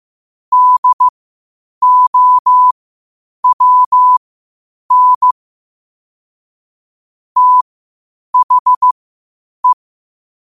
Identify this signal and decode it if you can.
Morse code, heard as a keyed tone, about 15 wpm, spaced as for 8 wpm, 1000 Hz. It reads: DOWN THE
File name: qotdfiles / 8wpm